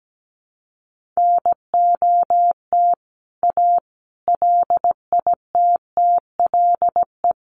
Morse code, heard as a keyed tone, about 17 wpm, 700 Hz. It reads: NOT A LITTLE
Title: A Sound Of Thunder 17wpm